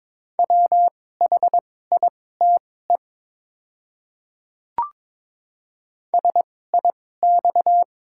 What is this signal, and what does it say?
Morse code, heard as a keyed tone, about 21 wpm, spaced as for 16 wpm, 690 Hz. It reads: WHITE SIX
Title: 2x Character Spacing - Top 500 Words - 22wpm